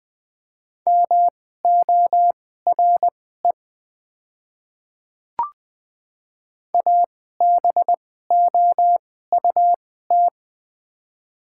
Morse code, heard as a keyed tone, about 21 wpm, spaced as for 15 wpm, 700 Hz. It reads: MORE ABOUT
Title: 2x Character Spacing - Top 100 Words - 20wpm